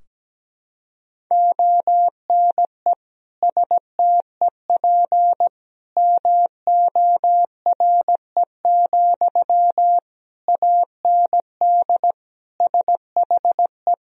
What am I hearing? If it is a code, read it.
Morse code, heard as a keyed tone, about 17 wpm, 700 Hz. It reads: ONE STEP MORE, AND SHE
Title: Sentences from Top 1000 Words - Review Part 3 - 17wpm